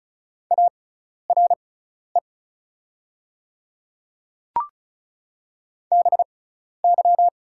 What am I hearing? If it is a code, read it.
Morse code, heard as a keyed tone, about 36 wpm, spaced as for 12 wpm, 690 Hz. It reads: ARE BY
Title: 6x Character Spacing - Top 300 Words - 35wpm